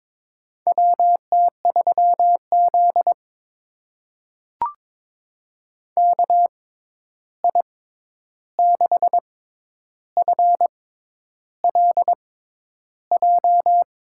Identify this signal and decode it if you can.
Morse code, heard as a keyed tone, about 22 wpm, spaced as for 10 wpm, 700 Hz. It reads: WT3Z KI6FLJ
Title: USA Call Signs - 6x Character Spacing - 22wpm